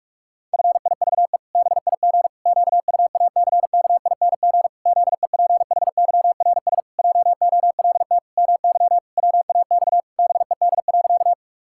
Morse code, heard as a keyed tone, about 41 wpm, 700 Hz. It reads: FIVE BIG QUACKING ZEPHYRS JOLT MY WAX BED.
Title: Pangram Challenge v2 - 45wpm